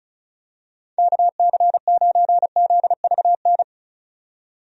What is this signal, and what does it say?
Morse code, heard as a keyed tone, about 34 wpm, 700 Hz. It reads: KC9ZVN
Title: USA Call Signs - 6 Characters - 35wpm